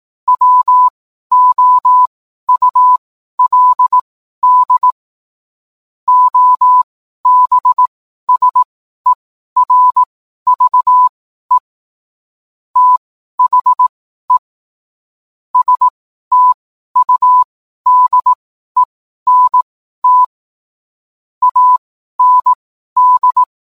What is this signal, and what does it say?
Morse code, heard as a keyed tone, about 18 wpm, spaced as for 12 wpm, 1000 Hz. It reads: WOULD OBSERVE THE STUDENT AND
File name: cwblog / cw mp3 directory / mem morse 12